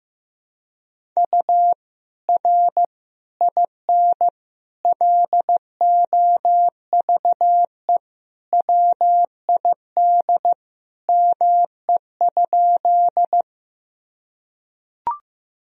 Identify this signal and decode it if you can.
Morse code, heard as a keyed tone, about 15 wpm, 700 Hz. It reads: U R IN LOVE WID ME?
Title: Sentences from Top 400 Words and Abbreviations 15wpm